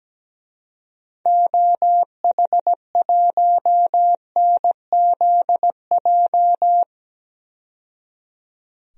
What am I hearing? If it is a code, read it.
Morse code, heard as a keyed tone, about 17 wpm, 700 Hz. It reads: OH1NZJ
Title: DXCC-100 Call Signs - 6 Characters - 17wpm